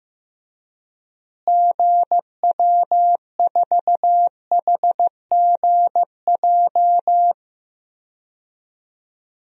Morse code, heard as a keyed tone, about 15 wpm, 700 Hz. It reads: GW4HGJ
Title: DXCC-100 Call Signs - 6 Characters - Repeated 2x - 15wpm